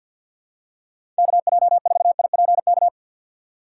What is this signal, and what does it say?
Morse code, heard as a keyed tone, about 52 wpm, 700 Hz. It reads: KJ4IPK
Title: USA Call Signs - 6 Characters - 3x Character Spacing - 50wpm